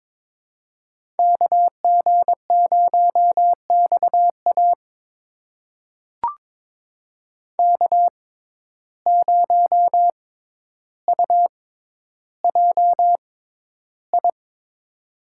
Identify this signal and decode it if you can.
Morse code, heard as a keyed tone, about 22 wpm, spaced as for 11 wpm, 700 Hz. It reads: KG0XA K0UJI
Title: USA Call Signs - 5 Characters - 6x Character Spacing - 22wpm